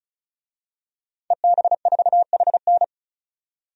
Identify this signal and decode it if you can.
Morse code, heard as a keyed tone, about 36 wpm, 700 Hz. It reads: EB4HN
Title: DXCC-100 Call Signs - 5 Characters - 35wpm